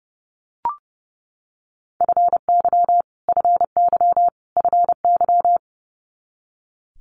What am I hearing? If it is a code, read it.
Morse code, heard as a keyed tone, about 30 wpm, 700 Hz. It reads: FY FY FY
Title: Multi-letter Phonemes Repeated 3x 30wpm